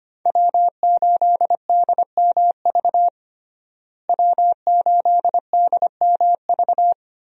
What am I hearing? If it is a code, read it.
Morse code, heard as a keyed tone, about 25 wpm, 700 Hz. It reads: W8DMV W8DMV
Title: USA Call Signs - 5 Characters - Repeated 2x - 25wpm